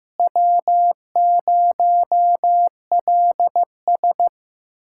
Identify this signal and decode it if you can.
Morse code, heard as a keyed tone, about 16 wpm, 700 Hz. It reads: W0LS
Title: USA Callsign DE Callsign - 4 Characters - 15wpm